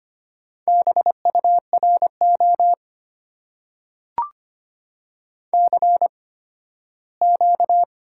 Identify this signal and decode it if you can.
Morse code, heard as a keyed tone, about 25 wpm, spaced as for 15 wpm, 700 Hz. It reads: BURO CQ
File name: Top 90 QSO Elements - 8x Character Spacing - 25wpm